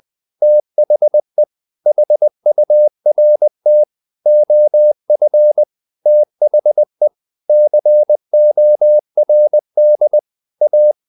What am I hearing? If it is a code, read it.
Morse code, heard as a keyed tone, about 20 wpm, 600 Hz. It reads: THE HURT OF THE CORD A
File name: the old man and the sea / the old man and the sea part 1 20wpm